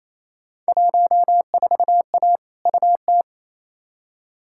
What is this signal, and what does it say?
Morse code, heard as a keyed tone, about 28 wpm, 700 Hz. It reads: 14A UT
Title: Field Day Exchanges Encore - 28wpm